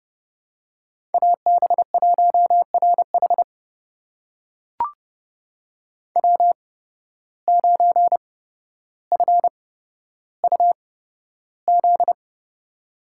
Morse code, heard as a keyed tone, about 30 wpm, spaced as for 13 wpm, 700 Hz. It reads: AB1RH W9FUZ
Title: USA Call Signs - 5 Characters - 8x Character Spacing - 30wpm